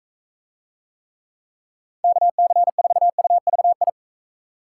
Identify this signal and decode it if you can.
Morse code, heard as a keyed tone, about 38 wpm, 700 Hz. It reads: KC4UVI